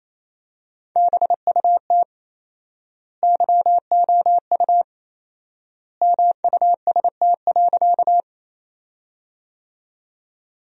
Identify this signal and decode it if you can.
Morse code, heard as a keyed tone, about 28 wpm, 700 Hz. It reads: BUT YOU MUST.
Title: 4x Word-Spacing - Sentences from Top 200 Words - 28wpm